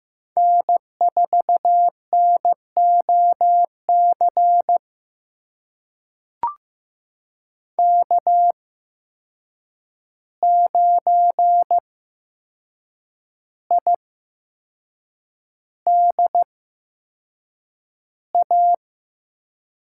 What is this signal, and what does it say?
Morse code, heard as a keyed tone, about 15 wpm, spaced as for 6 wpm, 700 Hz. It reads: N4NOC K9IDA
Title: USA Call Signs - 5 Characters - 8x Character Spacing - 15wpm